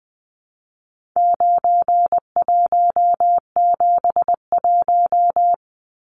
Morse code, heard as a keyed tone, about 20 wpm, 700 Hz. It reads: 9171
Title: Numbers - 4-Digit 20wpm